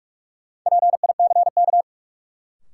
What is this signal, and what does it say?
Morse code, heard as a keyed tone, about 47 wpm, 700 Hz. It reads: PICK
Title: Sentences from Top 1000 Words - New Word Drill - 45wpm